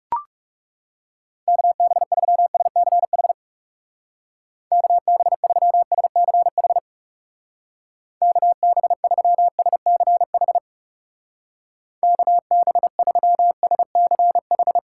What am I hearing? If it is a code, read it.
Morse code, heard as a keyed tone, about 36 wpm, 700 Hz. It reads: KB3SCH KB3SCH KB3SCH KB3SCH
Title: Speed Racing - USA 2x3 Call Signs 30wpm